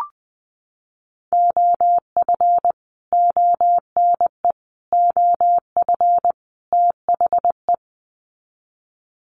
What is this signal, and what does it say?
Morse code, heard as a keyed tone, about 20 wpm, 700 Hz. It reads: OF ONE OF THE
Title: Sets of 4 Words 20wpm